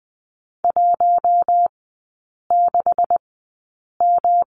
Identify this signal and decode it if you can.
Morse code, heard as a keyed tone, about 20 wpm, spaced as for 8 wpm, 700 Hz. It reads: 16M
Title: 178 Lesson 29e - Slash - Three Characters 20wpm